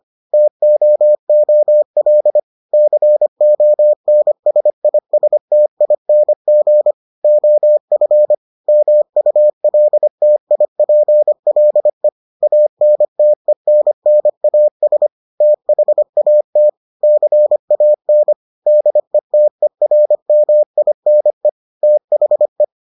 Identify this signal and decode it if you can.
Morse code, heard as a keyed tone, about 25 wpm, 600 Hz. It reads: TOOL CONSISTING OF MULTIPLE ANTENNAS THAT CAN DETERMINE THE